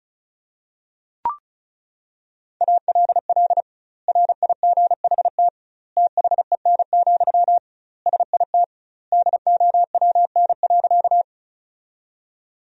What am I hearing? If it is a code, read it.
Morse code, heard as a keyed tone, about 36 wpm, 700 Hz. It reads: ALL RIGHT THEN, SIT DOWN.
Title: 2x Word-Spacing - Sentences from Top 500 Words - 35wpm